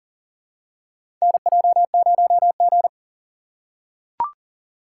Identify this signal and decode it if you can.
Morse code, heard as a keyed tone, about 40 wpm, 700 Hz. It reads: NJ0G
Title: USA Call Signs - 4 Characters - 4x Character Spacing - 40wpm